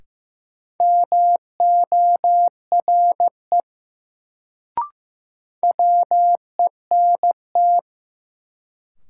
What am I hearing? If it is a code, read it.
Morse code, heard as a keyed tone, about 15 wpm, 700 Hz. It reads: MORE WENT
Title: Sets of 2 Words - New Word Drill - 15wpm